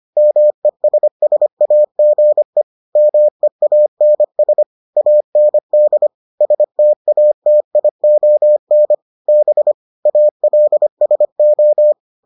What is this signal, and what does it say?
Morse code, heard as a keyed tone, about 25 wpm, 600 Hz. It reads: MESSAGE MEANS AND STATION B ALSO